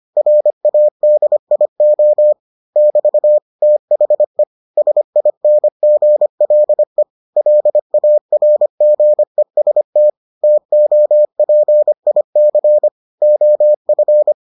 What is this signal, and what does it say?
Morse code, heard as a keyed tone, about 25 wpm, 600 Hz. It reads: RADIO <BT> THE SINGLE LARGEST TOPIC OF